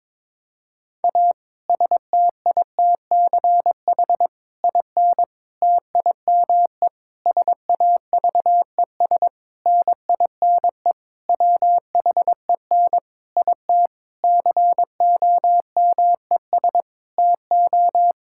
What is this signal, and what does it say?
Morse code, heard as a keyed tone, about 22 wpm, 700 Hz. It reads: A STITCH IN TIME SAVES NINE WHEN IT COMES TO
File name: Difficult Sentences with Hints - 22wpm